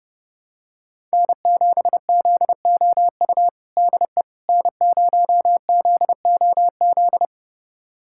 Morse code, heard as a keyed tone, about 32 wpm, 700 Hz. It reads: N7ZOU DE N0ZOZ
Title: USA Callsign DE Callsign - 5 Characters - 30wpm